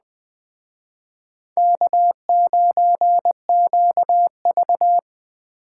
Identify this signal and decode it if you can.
Morse code, heard as a keyed tone, about 20 wpm, 700 Hz. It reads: K9QV
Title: Speed Racing - USA Call Signs 17wpm